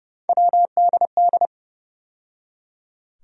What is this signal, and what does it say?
Morse code, heard as a keyed tone, about 30 wpm, 700 Hz. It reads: WDD
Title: USA Call Sign Postfixes - 3 Letters 30wpm